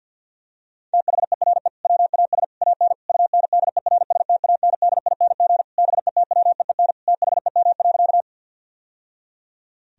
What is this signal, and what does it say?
Morse code, heard as a keyed tone, about 50 wpm, 700 Hz. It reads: THERE WAS AN UNDERSTANDING BETWEEN THEM.
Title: Sentences from Top 1000 Words - Review Part 3 - 50wpm